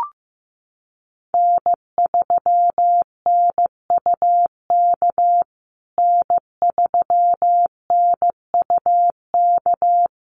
Morse code, heard as a keyed tone, about 15 wpm, 700 Hz. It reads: N3NUK N3NUK
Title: USA Call Signs Repeated 2x 15wpm